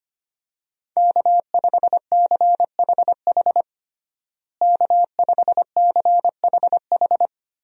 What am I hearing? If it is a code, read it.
Morse code, heard as a keyed tone, about 25 wpm, 700 Hz. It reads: K5CHH K5CHH
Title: USA Call Signs - 5 Characters - Repeated 3x - 25wpm